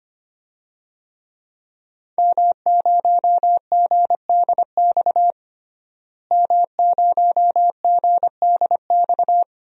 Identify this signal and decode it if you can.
Morse code, heard as a keyed tone, about 25 wpm, 700 Hz. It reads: M0GDX M0GDX
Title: DXCC-100 Call Signs - 5 Characters - Repeated 2x - 25wpm